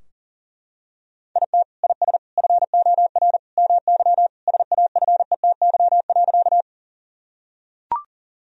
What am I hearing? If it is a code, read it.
Morse code, heard as a keyed tone, about 40 wpm, 700 Hz. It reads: IT IS FOR MY SAFETY.
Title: Sentences from Top 1000 Words - Review Part 4 - 40wpm